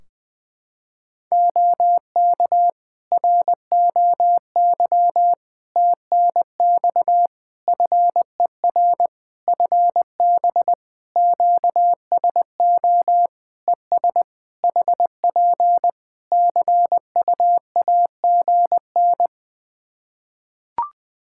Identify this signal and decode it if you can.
Morse code, heard as a keyed tone, about 20 wpm, 700 Hz. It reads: OK ROY TNX FER FB QSO ES HP CUAGN
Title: Practice QSO - Exchange 4 - Ending - 20wpm